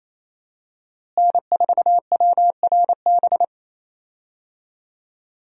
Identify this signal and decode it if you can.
Morse code, heard as a keyed tone, about 27 wpm, 700 Hz. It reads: N4WRB
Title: USA Call Signs - 5 Characters - Repeated 3x - 28wpm